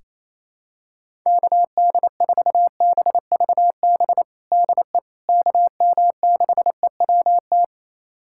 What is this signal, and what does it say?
Morse code, heard as a keyed tone, about 28 wpm, 700 Hz. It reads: KD4BVB DE KM6EWT
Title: USA Callsign DE Callsign - 6 Characters - 28wpm